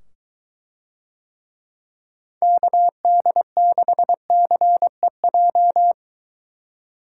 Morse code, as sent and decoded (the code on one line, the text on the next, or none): -.- -.. -.... -.-. .---
KD6CEJ